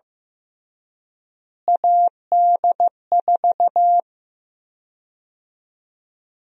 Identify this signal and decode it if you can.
Morse code, heard as a keyed tone, about 15 wpm, 700 Hz. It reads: AD4
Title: USA Call Sign Prefixes - Repeated 2x - 15wpm